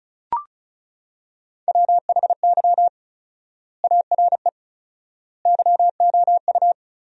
Morse code, heard as a keyed tone, about 36 wpm, 700 Hz. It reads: WHY ARE YOU
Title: 4x Word-Spacing - Sentences from Top 200 Words - 35wpm